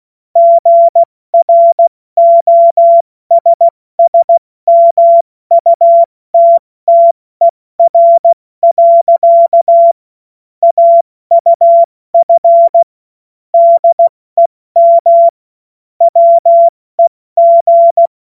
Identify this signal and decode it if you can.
Morse code, heard as a keyed tone, about 16 wpm, 680 Hz. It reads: GROSSMUTTER. AUF DEM WEG
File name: Morsekurs / Witze / Witz 23. August 2022 70 F80